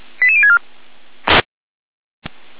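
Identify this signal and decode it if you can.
Five-tone selective call (ZVEI), heard as quick runs of five tones, tones of about 70 ms. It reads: 88064
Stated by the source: ZVEI five-tone call received by SDR, from the feuerwehr-orlamuende.org (fire brigade Orlamünde) server.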